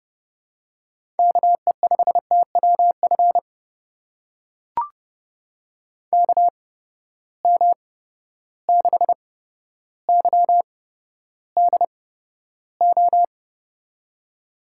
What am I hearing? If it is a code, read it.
Morse code, heard as a keyed tone, about 30 wpm, spaced as for 13 wpm, 700 Hz. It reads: KE5TWF KM6YDO
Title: USA Call Signs - 6 Characters - 8x Character Spacing - 30wpm